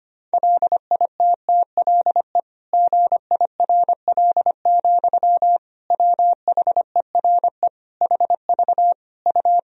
Morse code, heard as a keyed tone, about 25 wpm, 700 Hz. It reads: LITTLE GIRL, WHERE HV U
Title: Sentences from Top 400 Words and Abbreviations 25wpm